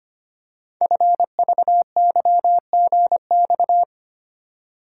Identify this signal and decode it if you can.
Morse code, heard as a keyed tone, about 25 wpm, 700 Hz. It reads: FVYGX
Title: Code Groups - 5 Letters - 25wpm